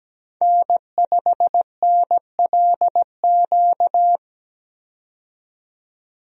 Morse code, heard as a keyed tone, about 17 wpm, 700 Hz. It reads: N5NLQ
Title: USA Call Signs - 5 Characters - 17wpm